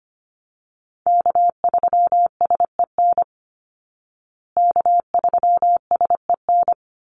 Morse code, heard as a keyed tone, about 25 wpm, 700 Hz. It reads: K3SEN K3SEN
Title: Call Signs Repeated 3x - 25wpm